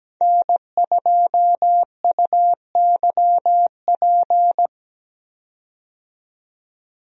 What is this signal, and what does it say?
Morse code, heard as a keyed tone, about 17 wpm, 700 Hz. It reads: N2UYP
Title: USA Call Signs - 5 Characters - 17wpm